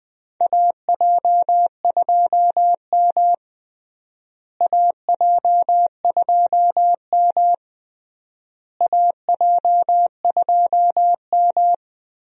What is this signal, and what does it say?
Morse code, heard as a keyed tone, about 20 wpm, 700 Hz. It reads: AJ2M AJ2M AJ2M
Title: USA Call Signs - 4 Characters - Repeated 3x - 20wpm